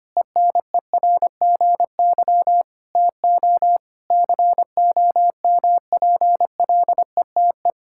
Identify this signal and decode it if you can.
Morse code, heard as a keyed tone, about 25 wpm, 700 Hz. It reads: ENERGY TO COMPLETE
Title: Sentences from Top 1000 Words - Review Part 6 - 25wpm